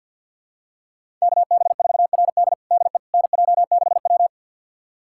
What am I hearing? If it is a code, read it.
Morse code, heard as a keyed tone, about 50 wpm, 700 Hz. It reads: KB4RD DE NJ6W
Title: USA Callsign DE Callsign 50wpm